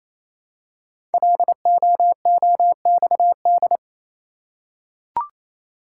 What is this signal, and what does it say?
Morse code, heard as a keyed tone, about 27 wpm, 700 Hz. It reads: LOOXD
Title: Code Groups - 5 Letter:Numbers - 28wpm